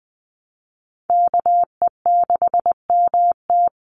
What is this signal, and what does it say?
Morse code, heard as a keyed tone, about 20 wpm, 690 Hz. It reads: KE6MT
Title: NA SOTA Chaser Callsigns 20wpm